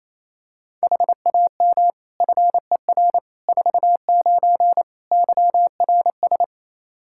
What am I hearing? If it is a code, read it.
Morse code, heard as a keyed tone, about 27 wpm, 700 Hz. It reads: HAM FER 49 YRS
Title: Practice QSO - Exchange 3 - More Info - 28wpm